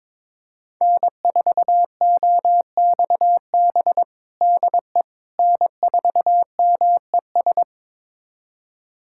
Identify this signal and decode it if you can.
Morse code, heard as a keyed tone, about 22 wpm, 700 Hz. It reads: N4OXB DE N4MES
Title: USA Callsign DE Callsign - 5 Characters - 22wpm